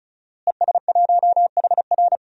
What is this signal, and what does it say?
Morse code, heard as a keyed tone, about 34 wpm, 700 Hz. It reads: ES1HR
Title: DXCC-100 Call Signs - 5 Characters - Repeated 3x - 35wpm